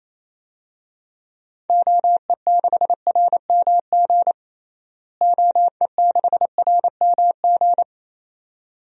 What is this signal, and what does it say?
Morse code, heard as a keyed tone, about 28 wpm, 700 Hz. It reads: OE6RMG OE6RMG
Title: DXCC-100 Call Signs - 6 Characters - Repeated 2x - 28wpm